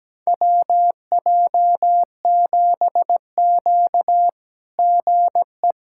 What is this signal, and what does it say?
Morse code, heard as a keyed tone, about 17 wpm, 700 Hz. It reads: WJ7Q GE
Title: SOTA QSO Practice 17wpm